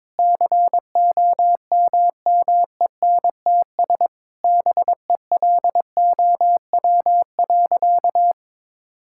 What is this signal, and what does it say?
Morse code, heard as a keyed tone, about 22 wpm, 700 Hz. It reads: COMMENTS BELOW.
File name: Sentences with 7-Word Phrases - 22wpm